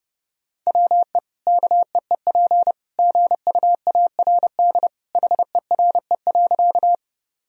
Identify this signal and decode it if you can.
Morse code, heard as a keyed tone, about 30 wpm, 700 Hz. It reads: WE KEEP GUARD HERE.
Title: Sentences from Top 1000 Words - Review Part 4 - 30wpm